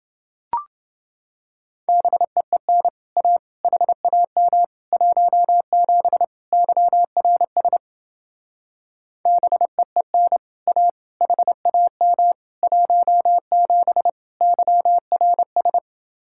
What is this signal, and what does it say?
Morse code, heard as a keyed tone, about 28 wpm, 700 Hz. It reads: BEEN A HAM 17 YRS BEEN A HAM 17 YRS